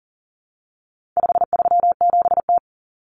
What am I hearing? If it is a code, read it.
Morse code, heard as a keyed tone, about 40 wpm, 700 Hz. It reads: 537T
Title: Common Cut Numbers - 4-Digit 40wpm